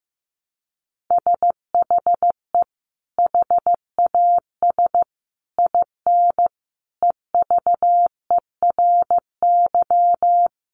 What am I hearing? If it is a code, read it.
Morse code, heard as a keyed tone, about 15 wpm, 700 Hz. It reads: SHE HAS IN EVERY
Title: Sentences from Top 1300 Words 15wpm